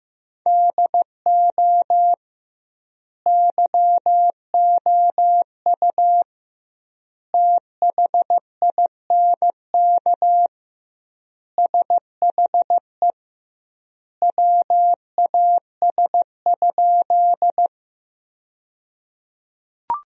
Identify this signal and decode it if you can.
Morse code, heard as a keyed tone, about 15 wpm, 700 Hz. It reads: DO YOU THINK SHE WAS?
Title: 2x Word-Spacing - Sentences from Top 200 Words - 15wpm